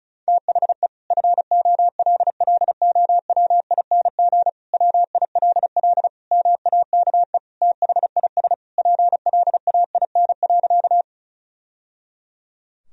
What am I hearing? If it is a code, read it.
Morse code, heard as a keyed tone, about 36 wpm, 700 Hz. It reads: THE FOLLOWING WILL MAKE THIS PLAIN.
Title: Sentences from Top 1000 Words - Review Part 4 - 35wpm